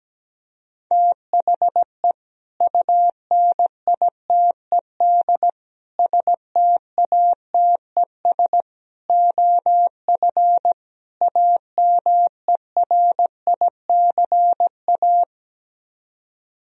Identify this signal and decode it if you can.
Morse code, heard as a keyed tone, about 17 wpm, 700 Hz. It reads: THE UNITED STATES OF AMERICA
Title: Speed-Racing - Sets of 5 Words 17wpm